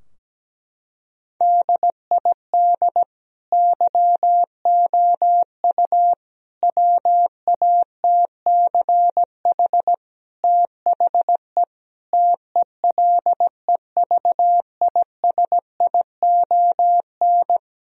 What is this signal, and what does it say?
Morse code, heard as a keyed tone, about 17 wpm, 700 Hz. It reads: DID YOU WATCH THE TELEVISION
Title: Sentences from Top 1000 Words - Review Part 4 - 17wpm